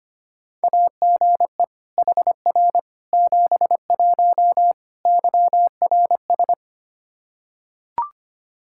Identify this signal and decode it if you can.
Morse code, heard as a keyed tone, about 25 wpm, 700 Hz. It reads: AGE HR 71 YRS
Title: Practice QSO - Exchange 3 - More Info - 25wpm